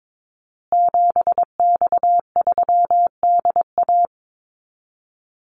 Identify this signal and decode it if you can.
Morse code, heard as a keyed tone, about 22 wpm, 700 Hz. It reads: 7X3DA
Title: Call Signs Repeated 3x - 22wpm